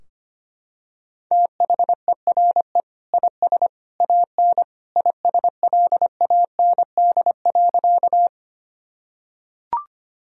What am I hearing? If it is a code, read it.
Morse code, heard as a keyed tone, about 25 wpm, 700 Hz. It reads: THERE IS AN ISLAND.
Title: Sentences from Top 1000 Words - Review Part 3 - 25wpm